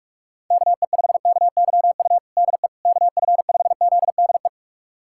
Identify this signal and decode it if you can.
Morse code, heard as a keyed tone, about 47 wpm, 700 Hz. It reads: KE5KYU DE KF5ZDE
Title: USA Callsign DE Callsign 45wpm